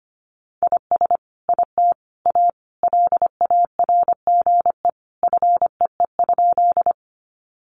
Morse code, heard as a keyed tone, about 25 wpm, 700 Hz. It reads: IS IT A LARGE FEE?
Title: Sentences from Top 1500 Words 25wpm